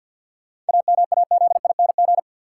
Morse code, heard as a keyed tone, about 53 wpm, 700 Hz. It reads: AMAZING
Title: Sentences from Top 1500 Words - New Word Drill Alt - 50wpm